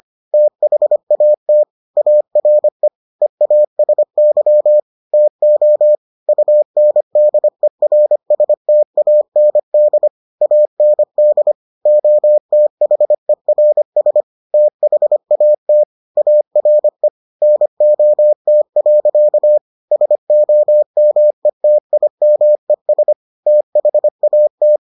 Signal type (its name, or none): Morse code